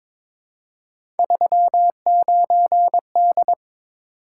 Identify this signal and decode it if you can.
Morse code, heard as a keyed tone, about 22 wpm, 700 Hz. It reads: 39D
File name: Three Letter-Number Combinations - 22wpm